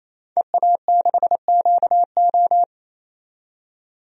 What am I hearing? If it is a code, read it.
Morse code, heard as a keyed tone, about 28 wpm, 700 Hz. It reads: EA6QO
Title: DXCC-100 Call Signs - 5 Characters - 28wpm